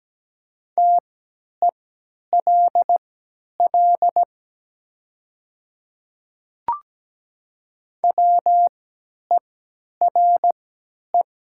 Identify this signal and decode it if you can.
Morse code, heard as a keyed tone, about 17 wpm, spaced as for 10 wpm, 700 Hz. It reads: TELL WERE
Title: 3x Character Spacing - Top 300 Words - 17wpm